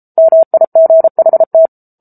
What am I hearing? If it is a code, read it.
Morse code, heard as a keyed tone, about 34 wpm, 650 Hz. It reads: MIGHT
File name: Book Two / 07-Chapter 7 - 33 WPM